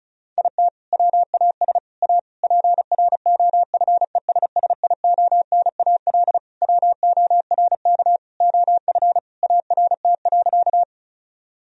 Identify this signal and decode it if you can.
Morse code, heard as a keyed tone, about 36 wpm, 700 Hz. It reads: IT WAS A PROFESSIONAL WORK OF ART.
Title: Sentences from Top 1000 Words - Review Part 4 - 35wpm